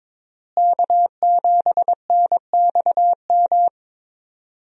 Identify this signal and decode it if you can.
Morse code, heard as a keyed tone, about 22 wpm, 700 Hz. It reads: K7NXM
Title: USA Call Signs - 5 Characters - Repeated 2x - 22wpm